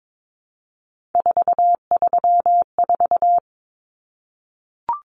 Morse code, heard as a keyed tone, about 22 wpm, 700 Hz. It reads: 434
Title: Common Cut Numbers - 3-Digit 22wpm